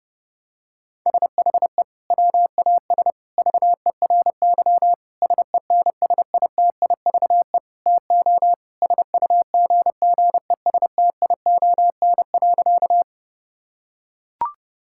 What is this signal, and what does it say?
Morse code, heard as a keyed tone, about 32 wpm, 700 Hz. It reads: SHE WAS VERY SENSITIVE TO SUGGESTION.
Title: Sentences from Top 1600 Words 30wpm